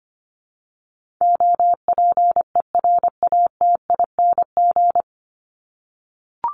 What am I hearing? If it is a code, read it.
Morse code, heard as a keyed tone, about 25 wpm, 700 Hz. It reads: OPERATING
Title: ING Suffix - Top 100 Words with Base Words 25wpm